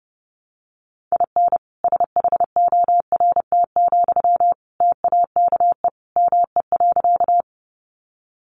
Morse code, heard as a keyed tone, about 30 wpm, 700 Hz. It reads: IN SHORT, TAKE ME.
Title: Sentences from Top 400 Words 30wpm